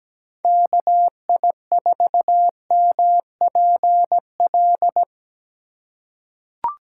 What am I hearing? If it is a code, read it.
Morse code, heard as a keyed tone, about 17 wpm, 700 Hz. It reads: KI4MPL
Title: USA Call Signs - 6 Characters - 4x Character Spacing - 17wpm